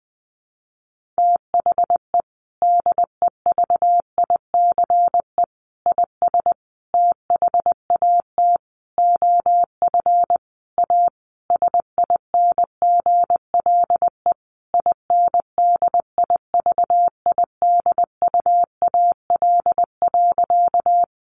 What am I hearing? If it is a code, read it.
Morse code, heard as a keyed tone, about 20 wpm, 700 Hz. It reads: THE DEVICE IS THAT OF A SINGLE INDIVIDUAL.
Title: Sentences from Top 1500 Words 20wpm